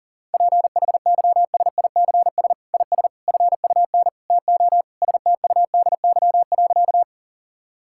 Morse code, heard as a keyed tone, about 40 wpm, 700 Hz. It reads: PHYSICS IS FUN TO STUDY.
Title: Sentences from Top 1000 Words - Review Part 5 - 40wpm